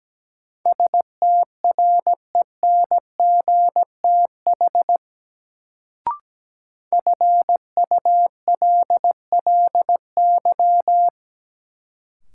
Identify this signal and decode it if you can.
Morse code, heard as a keyed tone, about 17 wpm, 700 Hz. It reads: STRENGTH FULLY
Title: Sentences from Top 1300 Words - New Word Drill Alt - 17wpm